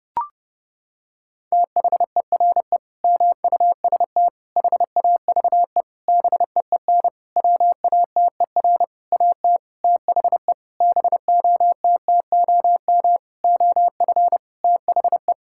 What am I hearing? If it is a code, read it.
Morse code, heard as a keyed tone, about 32 wpm, 700 Hz. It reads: THERE MUST HAVE BEEN WATER AT THE BOTTOM OF THE
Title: Sentences from Top 1000 Words - Review Part 6 - 30wpm